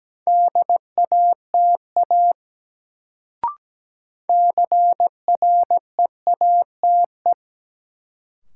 Morse code, heard as a keyed tone, about 17 wpm, 700 Hz. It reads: DATA CREATE
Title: Sentences from Top 1000 Words - Review Part 3 - New Word Drill - 17wpm